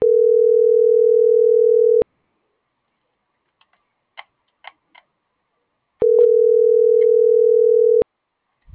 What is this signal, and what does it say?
Ringback tone (North America), 440 + 480 Hz.